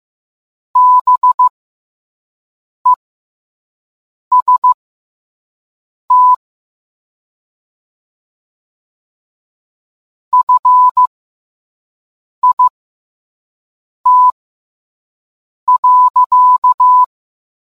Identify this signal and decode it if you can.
Morse code, heard as a keyed tone, about 15 wpm, spaced as for 5 wpm, 1000 Hz. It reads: BEST FIT.